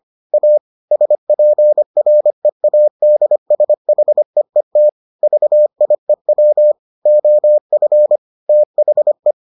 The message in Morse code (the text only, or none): A SPREADSHEET VIEW OF THE